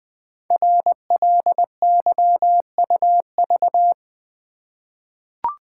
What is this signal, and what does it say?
Morse code, heard as a keyed tone, about 20 wpm, 700 Hz. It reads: RLYUV
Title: Code Groups - 5 Letters - 20wpm